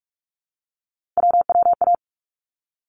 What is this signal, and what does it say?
Morse code, heard as a keyed tone, about 46 wpm, 700 Hz. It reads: WWA
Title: ARRL Sections 45wpm